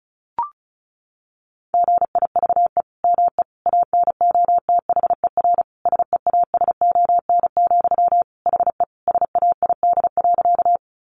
Morse code, heard as a keyed tone, about 34 wpm, 700 Hz. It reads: GIVE ME ANOTHER SEASON, HE SAID.
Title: Sentences from Top 1400 Words 35wpm